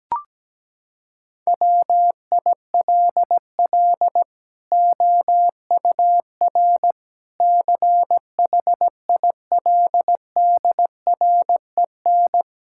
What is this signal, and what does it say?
Morse code, heard as a keyed tone, about 17 wpm, 700 Hz. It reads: WILL OUR CHILDREN
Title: Sentences from Top 1000 Words - Review Part 5 - 17wpm